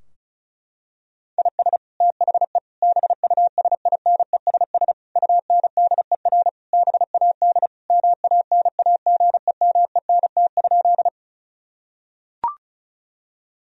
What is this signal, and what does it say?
Morse code, heard as a keyed tone, about 34 wpm, 700 Hz. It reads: IS THE BUSINESS UNDER BAD MANAGEMENT?
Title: Sentences from Top 1000 Words - Review Part 4 - 35wpm